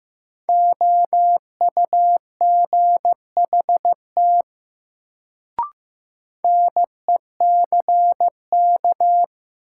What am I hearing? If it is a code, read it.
Morse code, heard as a keyed tone, about 15 wpm, 700 Hz. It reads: OUGHT NECK
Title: Sets of 5 Words Encore - New Word Drill - 15wpm